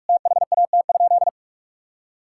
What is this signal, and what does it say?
Morse code, heard as a keyed tone, about 46 wpm, 700 Hz. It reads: THAT?
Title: Sentences from Top 1000 Words - Review Part 5 - 45wpm